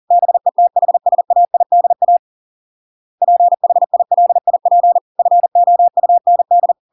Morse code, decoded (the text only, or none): BETHSAIDA PHILIP FOUND